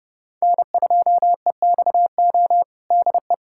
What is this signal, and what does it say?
Morse code, heard as a keyed tone, about 32 wpm, 700 Hz. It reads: N2EXO DE
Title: USA Callsign DE Callsign - 5 Characters - 30wpm